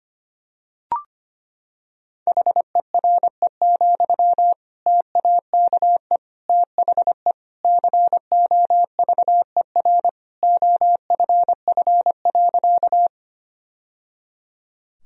Morse code, heard as a keyed tone, about 25 wpm, 700 Hz. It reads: HERE, TAKE THE COVER OFF.
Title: Sentences from Top 1000 Words - Review Part 1 - 25wpm